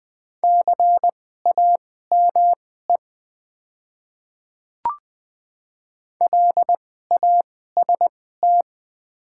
Morse code, heard as a keyed tone, about 20 wpm, spaced as for 14 wpm, 690 Hz. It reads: CAME LAST
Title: 2x Character Spacing - Top 300 Words - 20wpm